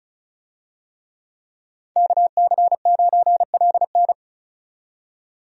Morse code, heard as a keyed tone, about 36 wpm, 700 Hz. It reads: KC9LN
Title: USA Call Signs - 5 Characters - 35wpm